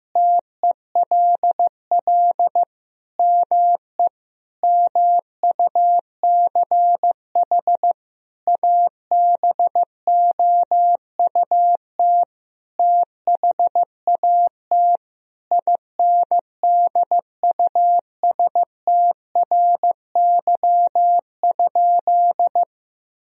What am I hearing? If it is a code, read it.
Morse code, heard as a keyed tone, about 15 wpm, 700 Hz. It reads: TELL ME MUCH ABOUT THAT INDUSTRY?
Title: Sentences from Top 1000 Words - Review Part 4 - 15wpm